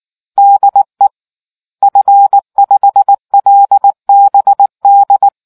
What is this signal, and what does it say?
Morse code, heard as a keyed tone, about 19 wpm, 800 Hz. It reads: DE F5LBD